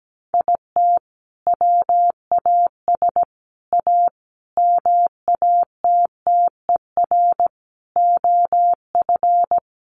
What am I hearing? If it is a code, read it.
Morse code, heard as a keyed tone, about 17 wpm, 700 Hz. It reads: IT WAS A MATTER OF
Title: Sentences from Top 1500 Words 17wpm